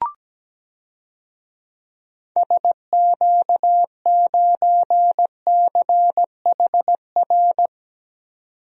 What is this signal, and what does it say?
Morse code, heard as a keyed tone, about 17 wpm, 700 Hz. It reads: SQ9CHR
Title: DXCC-100 Call Signs - 6 Characters - 17wpm